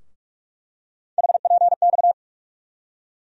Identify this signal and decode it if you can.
Morse code, heard as a keyed tone, about 47 wpm, 700 Hz. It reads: HPX
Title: USA Call Sign Postfixes - 3 Letters 45wpm